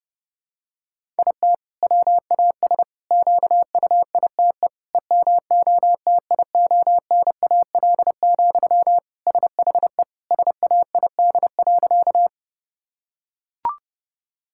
Morse code, heard as a keyed tone, about 32 wpm, 700 Hz. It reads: IT WAS QUITE EMOTIONAL, SHE SAID.
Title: Sentences from Top 1000 Words - Review Part 5 - 30wpm